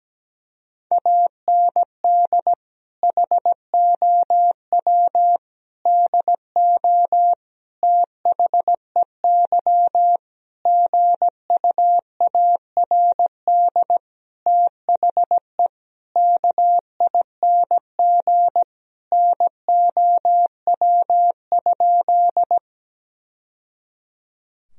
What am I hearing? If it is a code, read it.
Morse code, heard as a keyed tone, about 17 wpm, 700 Hz. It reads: AND HOW DO THEY GUARD THE KING NOW?
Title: Sentences from Top 1000 Words - Review Part 4 - 17wpm